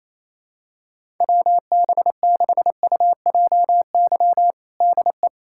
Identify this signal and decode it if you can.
Morse code, heard as a keyed tone, about 27 wpm, 700 Hz. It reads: WB6UJY DE
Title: USA Callsign DE Callsign 28wpm